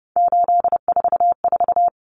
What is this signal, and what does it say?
Morse code, heard as a keyed tone, about 30 wpm, 700 Hz. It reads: 844
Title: Numbers - 3-Digit 30wpm